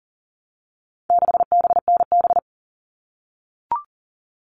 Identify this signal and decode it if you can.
Morse code, heard as a keyed tone, about 40 wpm, 700 Hz. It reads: 6BNB
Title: All Cut Numbers - 4-Digit 40wpm